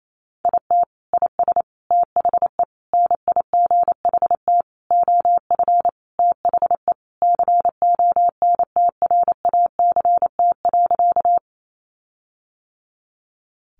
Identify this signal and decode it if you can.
Morse code, heard as a keyed tone, about 28 wpm, 700 Hz. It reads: IT IS THE NIGHT OF THE CONTRACT.
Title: Sentences from Top 1400 Words 28wpm